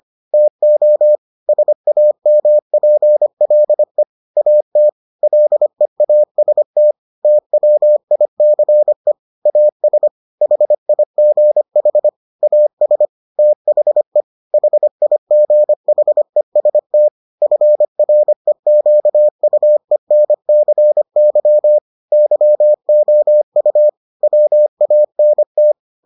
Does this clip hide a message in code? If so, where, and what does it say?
Yes: Morse code from 0.3 to 25.7 s, heard as a keyed tone, about 25 wpm, 600 Hz. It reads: TO SAMPLE AT LEAST TWICE AS HIGH AS THE HIGHEST FREQUENCY YOU WANT